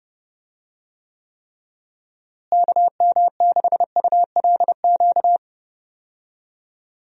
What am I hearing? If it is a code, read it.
Morse code, heard as a keyed tone, about 30 wpm, 700 Hz. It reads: KM6ULQ
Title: USA Call Signs - 6 Characters - Repeated 3x - 30wpm